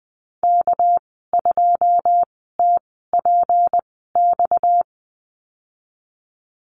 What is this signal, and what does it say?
Morse code, heard as a keyed tone, about 20 wpm, spaced as for 15 wpm, 700 Hz. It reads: K2TPX
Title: 231 Lesson 37g - Letter x - Call Signs 20wpm